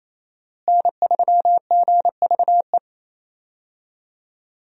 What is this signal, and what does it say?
Morse code, heard as a keyed tone, about 28 wpm, 700 Hz. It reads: N3GVE
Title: USA Call Signs - 5 Characters - Repeated 3x - 28wpm